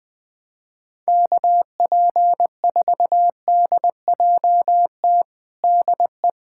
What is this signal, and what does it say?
Morse code, heard as a keyed tone, about 20 wpm, 700 Hz. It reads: KP4DJT DE
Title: USA Callsign DE Callsign - 6 Characters - 20wpm